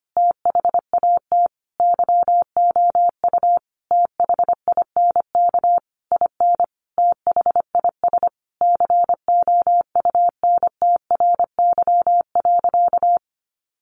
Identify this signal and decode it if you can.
Morse code, heard as a keyed tone, about 25 wpm, 700 Hz. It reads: THAT YOU THINK IN THIS COUNTRY.
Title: Sentences from Top 1200 Words 25wpm